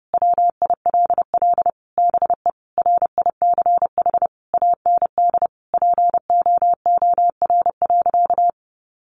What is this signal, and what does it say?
Morse code, heard as a keyed tone, about 30 wpm, 700 Hz. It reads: WILL BE RICH AND POOR.
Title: Sentences from Top 1100 Words 30wpm